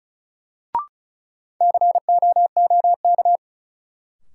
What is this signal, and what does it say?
Morse code, heard as a keyed tone, about 36 wpm, 700 Hz. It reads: COOK
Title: Sentences from Top 700 Words - New Word Drill - 35wpm